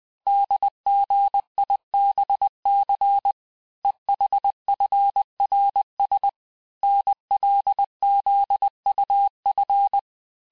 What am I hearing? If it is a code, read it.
Morse code, heard as a keyed tone, about 20 wpm, 800 Hz. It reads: DGIBC EHFRS NLZUF
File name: cours-858-texte-chiffre-vitesse-19mpm